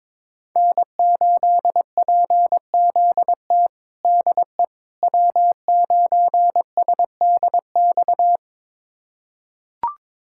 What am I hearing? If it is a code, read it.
Morse code, heard as a keyed tone, about 22 wpm, 700 Hz. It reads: N8PZT DE W9SDX
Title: USA Callsign DE Callsign - 5 Characters - 22wpm